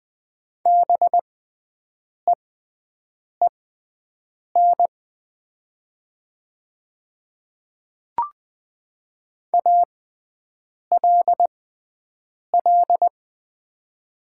Morse code, heard as a keyed tone, about 20 wpm, spaced as for 7 wpm, 700 Hz. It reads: BEEN ALL
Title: 6x Character Spacing - Top 100 Words - 20wpm